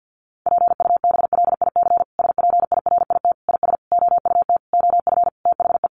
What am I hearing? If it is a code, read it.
Morse code, heard as a keyed tone, about 50 wpm, 700 Hz. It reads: PUBLIC SPIRIT IS OUT OF THE
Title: Sentences from Top 1400 Words 50wpm